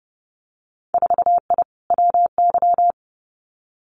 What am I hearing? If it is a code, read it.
Morse code, heard as a keyed tone, about 30 wpm, 700 Hz. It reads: 4I WY
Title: Winter Field Day Exchanges Encore 30wpm